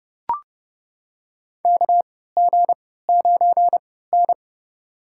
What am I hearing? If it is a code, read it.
Morse code, heard as a keyed tone, about 30 wpm, spaced as for 17 wpm, 700 Hz. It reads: KG9N
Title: USA Call Signs - 4 Characters - 3x Character Spacing - 30wpm